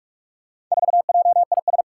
Morse code, heard as a keyed tone, about 47 wpm, 700 Hz. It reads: 4JIS
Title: Four Challenging Letter-Number Combinations 45wpm